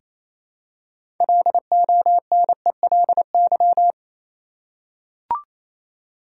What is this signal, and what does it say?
Morse code, heard as a keyed tone, about 28 wpm, 700 Hz. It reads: LONELY
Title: Six Letter Words - 28wpm